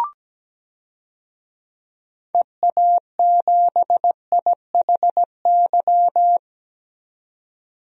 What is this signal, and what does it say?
Morse code, heard as a keyed tone, about 17 wpm, 700 Hz. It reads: EA7IHY
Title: DXCC-100 Call Signs - 6 Characters - 17wpm